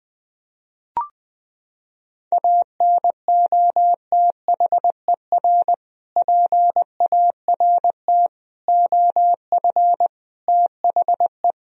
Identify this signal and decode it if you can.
Morse code, heard as a keyed tone, about 20 wpm, 700 Hz. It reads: ANOTHER PART OF THE
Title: Sentences from Top 1000 Words - Review Part 2 - 20wpm